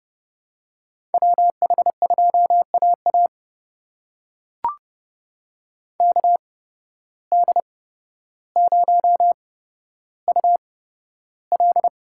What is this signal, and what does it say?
Morse code, heard as a keyed tone, about 30 wpm, spaced as for 13 wpm, 700 Hz. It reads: WH2AA KD0UL
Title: USA Call Signs - 8x Character Spacing - 30wpm